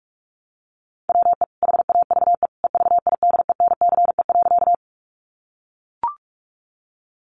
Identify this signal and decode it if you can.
Morse code, heard as a keyed tone, about 44 wpm, 700 Hz. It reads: WE HAVE EVIDENCE.
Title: Sentences from Top 1300 Words 45wpm